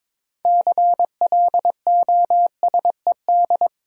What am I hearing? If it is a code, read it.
Morse code, heard as a keyed tone, about 22 wpm, 700 Hz. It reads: CLOSED
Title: Six Letter Words - 22wpm